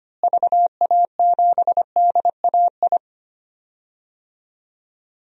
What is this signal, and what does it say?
Morse code, heard as a keyed tone, about 25 wpm, 700 Hz. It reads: VA7DAI
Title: DXCC-100 Call Signs - 6 Characters - Repeated 3x - 25wpm